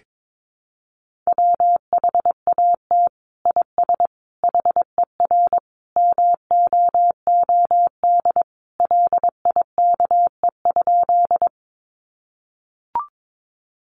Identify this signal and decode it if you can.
Morse code, heard as a keyed tone, about 22 wpm, 700 Hz. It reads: WHAT IS HER MOOD LIKE?
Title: Sentences from Top 1100 Words 22wpm